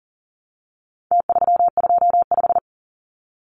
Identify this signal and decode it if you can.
Morse code, heard as a keyed tone, about 40 wpm, 700 Hz. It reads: T325
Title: Numbers - Contest Style - 4-Digit 40wpm